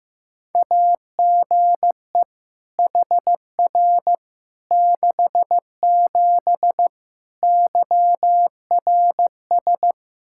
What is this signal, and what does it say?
Morse code, heard as a keyed tone, about 15 wpm, 700 Hz. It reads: AGE HR 67 YRS
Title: K1USN SST QSO Practice 15wpm